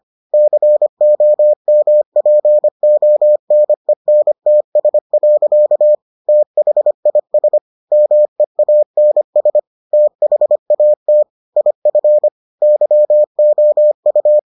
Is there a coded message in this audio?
Yes: Morse code from 0.3 to 14.4 s, heard as a keyed tone, about 25 wpm, 600 Hz. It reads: COMPONENTS. THIS MEANS THAT IF YOU